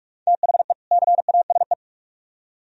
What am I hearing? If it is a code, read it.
Morse code, heard as a keyed tone, about 39 wpm, 700 Hz. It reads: THE CASE